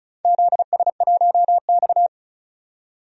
Morse code, heard as a keyed tone, about 36 wpm, 700 Hz. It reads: ZS1X
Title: DXCC-100 Call Signs - 4 Characters - 35wpm